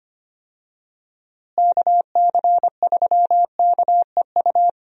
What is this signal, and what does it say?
Morse code, heard as a keyed tone, about 25 wpm, 700 Hz. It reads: KC3KEU
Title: USA Call Signs - 6 Characters - Repeated 3x - 25wpm